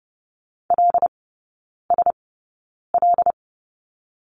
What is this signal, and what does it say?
Morse code, heard as a keyed tone, about 30 wpm, spaced as for 9 wpm, 700 Hz. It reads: LSL
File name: Lesson 17e - Letter u - Three Characters 30wpm